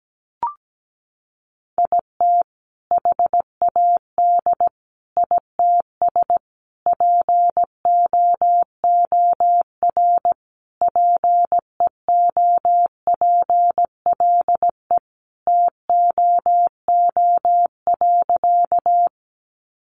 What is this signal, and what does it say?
Morse code, heard as a keyed tone, about 17 wpm, 700 Hz. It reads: IT HAD ITS POOR PEOPLE TOO.
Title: Sentences from Top 1000 Words 17wpm